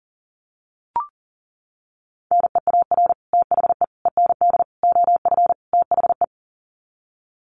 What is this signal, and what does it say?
Morse code, heard as a keyed tone, about 40 wpm, 700 Hz. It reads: NEAR THE END OF THE
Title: Sets of 5 Words Encore 40wpm